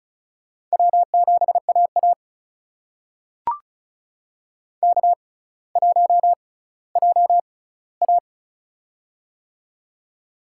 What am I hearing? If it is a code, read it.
Morse code, heard as a keyed tone, about 36 wpm, spaced as for 18 wpm, 700 Hz. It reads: W7AA K1JA
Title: USA Call Signs - 4 Characters - 6x Character Spacing - 35wpm